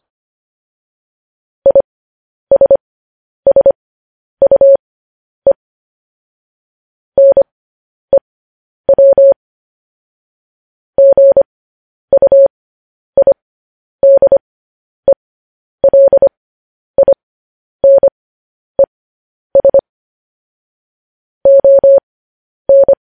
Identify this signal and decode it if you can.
Morse code, heard as a keyed tone, about 25 wpm, spaced as for 10 wpm, 550 Hz. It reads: ISSUE NEW GUIDELINES ON
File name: podcast / News headlines-1x-25x10